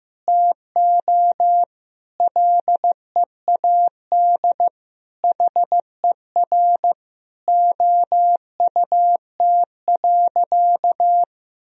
Morse code, heard as a keyed tone, about 15 wpm, 700 Hz. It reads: TO LEAD HER OUT.
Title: Sentences from Top 1000 Words - Review Part 2 - 15wpm